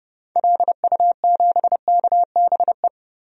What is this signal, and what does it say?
Morse code, heard as a keyed tone, about 30 wpm, 700 Hz. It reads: LU7KBE
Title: DXCC-100 Call Signs - 6 Characters - 30wpm